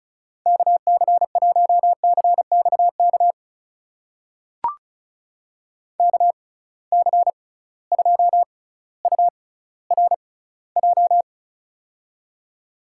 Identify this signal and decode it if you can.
Morse code, heard as a keyed tone, about 36 wpm, spaced as for 18 wpm, 700 Hz. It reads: KC1CXK KC2URJ
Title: USA Call Signs - 6 Characters - 6x Character Spacing - 35wpm